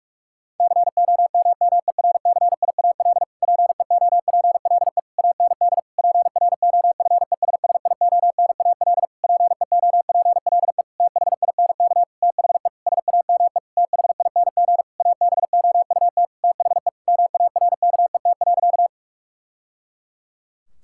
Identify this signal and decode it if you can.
Morse code, heard as a keyed tone, about 46 wpm, 700 Hz. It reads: COMMERCIAL PEOPLE AND PROFESSIONAL PEOPLE THINK THE SAME THING ABOUT THE MARKET.
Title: Sentences from Top 1000 Words - Review Part 5 - 45wpm